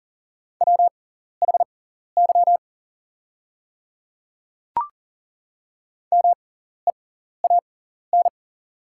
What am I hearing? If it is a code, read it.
Morse code, heard as a keyed tone, about 40 wpm, spaced as for 14 wpm, 700 Hz. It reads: WHY MEAN
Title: 6x Character Spacing - Top 500 Words - 40wpm